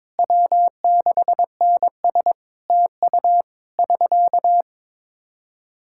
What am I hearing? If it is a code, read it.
Morse code, heard as a keyed tone, about 22 wpm, 700 Hz. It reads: W6NS TU <SK>
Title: Practice QSO - Exchange 4 - Ending - 22wpm